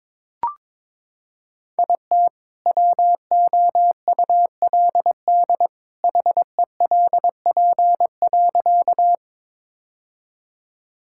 Morse code, heard as a keyed tone, about 22 wpm, 700 Hz. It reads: IT WOULD HELP.
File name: practice / 22wpm / Sentences with 2-Word Phrases - 22wpm